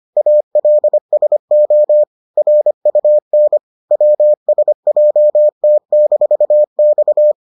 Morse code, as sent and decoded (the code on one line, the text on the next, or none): .- .-.. ... --- / .-. ..- -. / .-- ... .--- - -....- -..-
ALSO RUN WSJT-X